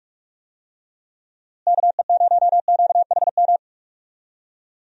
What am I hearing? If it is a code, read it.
Morse code, heard as a keyed tone, about 48 wpm, 700 Hz. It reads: KE0QHM
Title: USA Call Signs - 6 Characters - Repeated 2x - 45wpm